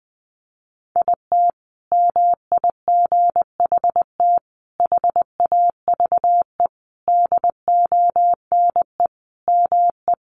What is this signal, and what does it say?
Morse code, heard as a keyed tone, about 20 wpm, 700 Hz. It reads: IT MIGHT HAVE DONE ME
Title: Sentences from Top 500 Words 20wpm